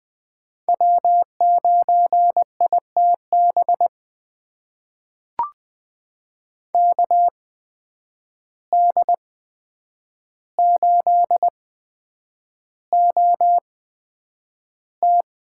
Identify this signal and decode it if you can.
Morse code, heard as a keyed tone, about 20 wpm, spaced as for 9 wpm, 700 Hz. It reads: W9ITB KD8OT
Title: USA Call Signs - 5 Characters - 8x Character Spacing - 20wpm